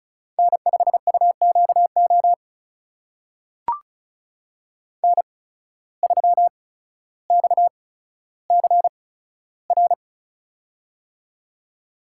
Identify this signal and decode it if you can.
Morse code, heard as a keyed tone, about 36 wpm, spaced as for 15 wpm, 700 Hz. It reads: N5UQO N3XCR
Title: USA Call Signs - 5 Characters - 8x Character Spacing - 35wpm